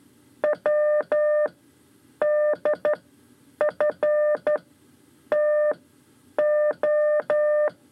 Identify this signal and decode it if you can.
Morse code, heard as a keyed tone, about 12 wpm, spaced as for 8 wpm, 600 Hz. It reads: WDFTO
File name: 5 Lezione CW 2